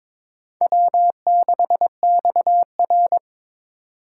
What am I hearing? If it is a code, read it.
Morse code, heard as a keyed tone, about 22 wpm, 700 Hz. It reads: W6XR
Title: K1USN SST QSO Practice - 22wpm